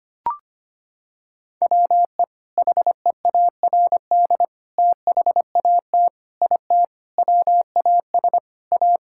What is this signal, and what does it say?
Morse code, heard as a keyed tone, about 25 wpm, 700 Hz. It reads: WE HEARD THAT IT WAS A
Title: Sentences from Top 1000 Words - Review Part 6 - 25wpm